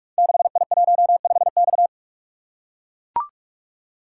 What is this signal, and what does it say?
Morse code, heard as a keyed tone, about 44 wpm, 700 Hz. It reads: 6I15X
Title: Code Groups - 5 Letter:Numbers - 45wpm